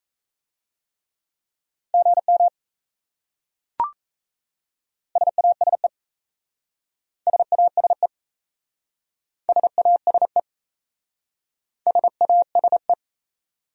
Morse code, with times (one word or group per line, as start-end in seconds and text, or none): 1.9-2.5 GM
5.2-5.9 SASE
7.3-8.1 SASE
9.5-10.4 SASE
11.9-12.9 SASE